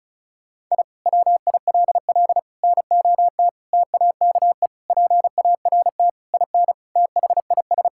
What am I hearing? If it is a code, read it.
Morse code, heard as a keyed tone, about 36 wpm, 700 Hz. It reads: I WILL NOT TAKE PART IN THIS
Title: Sentences from Top 1000 Words - Review Part 6 - 35wpm